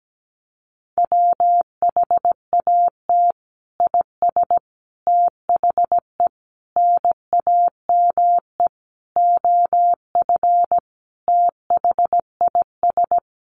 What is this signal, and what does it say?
Morse code, heard as a keyed tone, about 17 wpm, 700 Hz. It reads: WHAT IS THE NAME OF THIS
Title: Sentences from Top 1500 Words 17wpm